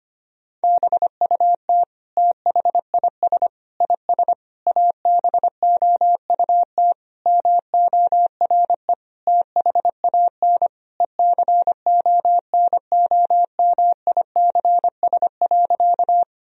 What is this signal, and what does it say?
Morse code, heard as a keyed tone, about 25 wpm, 700 Hz. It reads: BUT THIS IS ABOUT MORE THAN ECONOMICS.
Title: Sentences from Top 1000 Words - Review Part 3 - 25wpm